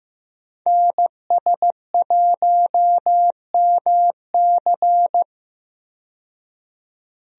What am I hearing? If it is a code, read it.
Morse code, heard as a keyed tone, about 15 wpm, 700 Hz. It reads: NS1MC
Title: USA Call Signs - 5 Characters - 15wpm